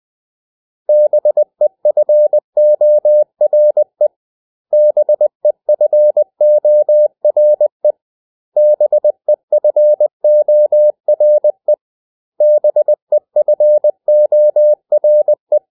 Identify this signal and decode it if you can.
Morse code, heard as a keyed tone, about 20 wpm, 600 Hz. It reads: BEFORE BEFORE BEFORE BEFORE
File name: cw / 100 Common English Words